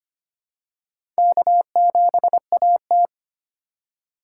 Morse code, heard as a keyed tone, about 25 wpm, 700 Hz. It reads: K7AT
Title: USA Call Signs - 4 Characters - 4x Character Spacing - 25wpm